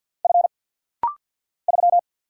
Morse code, heard as a keyed tone, about 50 wpm, 690 Hz. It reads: F 3
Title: Single Letter-Number Alt - Rapid-Fire 50wpm